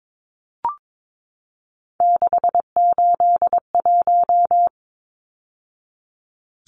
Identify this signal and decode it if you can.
Morse code, heard as a keyed tone, about 22 wpm, 700 Hz. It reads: 681
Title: Numbers - 3-Digit 22wpm